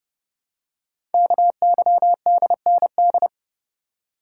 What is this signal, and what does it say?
Morse code, heard as a keyed tone, about 30 wpm, 700 Hz. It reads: KYDND
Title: Code Groups - 5 Letters - 30wpm